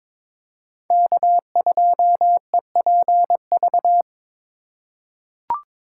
Morse code, heard as a keyed tone, about 22 wpm, 700 Hz. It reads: K2EPV
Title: USA Call Signs - 5 Characters - 3x Character Spacing - 22wpm